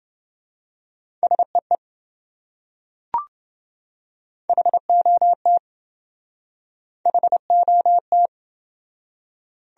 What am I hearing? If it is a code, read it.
Morse code, heard as a keyed tone, about 26 wpm, 700 Hz. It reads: SEE HOT HOT